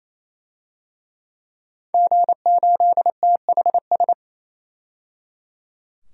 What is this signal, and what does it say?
Morse code, heard as a keyed tone, about 27 wpm, 700 Hz. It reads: G8THS
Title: DXCC-100 Call Signs - 5 Characters - 28wpm